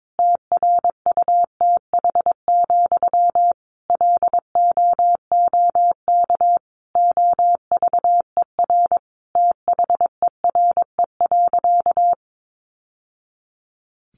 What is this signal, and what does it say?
Morse code, heard as a keyed tone, about 22 wpm, 700 Hz. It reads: TRUTH, LOOK OVER THERE.
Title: Sentences from Top 800 Words 22wpm